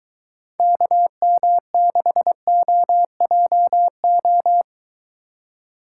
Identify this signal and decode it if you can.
Morse code, heard as a keyed tone, about 23 wpm, 700 Hz. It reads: KM6OJO